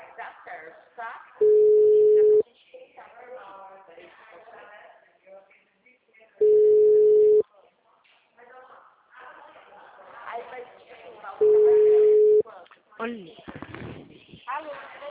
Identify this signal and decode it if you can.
Ringback tone (Europe, 425 Hz), with speech.